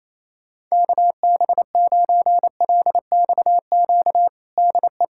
Morse code, heard as a keyed tone, about 27 wpm, 700 Hz. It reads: KB9LXQ DE
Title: USA Callsign DE Callsign - 6 Characters - 28wpm